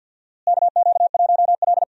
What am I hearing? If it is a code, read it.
Morse code, heard as a keyed tone, about 48 wpm, 700 Hz. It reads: KQ1L
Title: USA Call Signs - 4 Characters - Repeated 3x - 50wpm